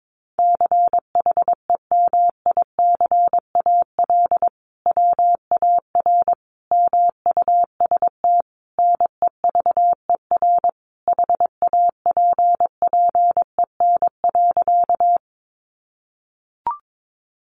Morse code, heard as a keyed tone, about 22 wpm, 700 Hz. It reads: CHEMICAL WAR MUST NEVER HAPPEN.
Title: Sentences from Top 1200 Words 22wpm